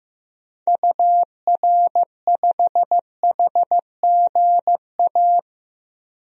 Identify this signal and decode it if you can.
Morse code, heard as a keyed tone, about 16 wpm, 700 Hz. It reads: UR5HGA
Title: DXCC-100 Call Signs - 6 Characters - 15wpm